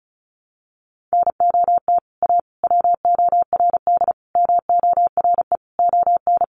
Morse code, heard as a keyed tone, about 34 wpm, 700 Hz. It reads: NOT A WORD MORE ON
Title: Sentences from Top 800 Words 35wpm